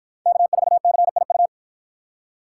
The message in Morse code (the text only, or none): K4CIU